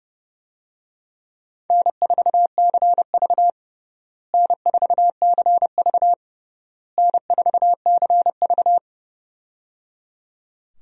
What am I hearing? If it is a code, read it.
Morse code, heard as a keyed tone, about 30 wpm, 700 Hz. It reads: N4CV N4CV N4CV
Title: USA Call Signs - 4 Characters - Repeated 3x - 30wpm